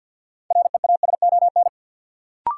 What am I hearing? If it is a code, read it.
Morse code, heard as a keyed tone, about 53 wpm, 700 Hz. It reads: REASON
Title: Sentences from Top 700 Words - New Word Drill - 50wpm